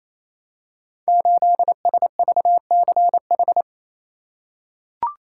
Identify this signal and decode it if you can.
Morse code, heard as a keyed tone, about 29 wpm, 700 Hz. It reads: 8SVCH
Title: Code Groups - 5 Letter:Numbers - 28wpm